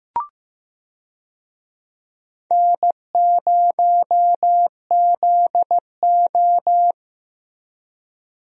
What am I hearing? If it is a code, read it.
Morse code, heard as a keyed tone, about 15 wpm, 700 Hz. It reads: N0ZO
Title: USA Call Signs - 4 Characters - Repeated 3x - 15wpm